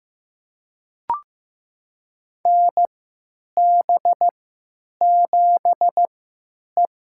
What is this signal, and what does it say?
Morse code, heard as a keyed tone, about 15 wpm, spaced as for 9 wpm, 700 Hz. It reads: NB7E
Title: USA Call Signs - 4 Characters - 3x Character Spacing - 15wpm